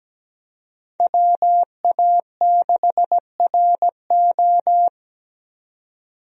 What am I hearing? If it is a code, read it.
Morse code, heard as a keyed tone, about 17 wpm, 700 Hz. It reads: WA6RO
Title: USA Call Signs - 5 Characters - 17wpm